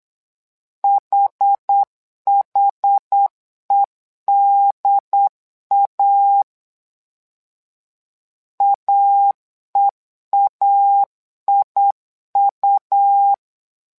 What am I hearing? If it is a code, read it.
Morse code, heard as a keyed tone, about 8 wpm, 800 Hz. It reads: HHEDA AEAIU
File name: LEZIONI / 2 UHDN / 2 UHD 8wpm 2258